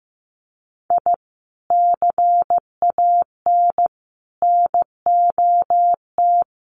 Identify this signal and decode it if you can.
Morse code, heard as a keyed tone, about 15 wpm, 700 Hz. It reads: I CAN NOT
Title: Sets of 5 Words Encore2 15wpm